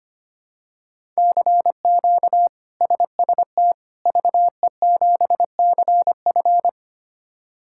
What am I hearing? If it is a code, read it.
Morse code, heard as a keyed tone, about 25 wpm, 700 Hz. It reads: CQ SST VE7CF
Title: K1USN SST QSO Practice - 25wpm